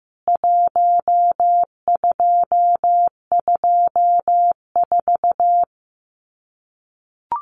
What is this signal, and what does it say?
Morse code, heard as a keyed tone, about 15 wpm, 700 Hz. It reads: 1224
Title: CWT Exchange 15wpm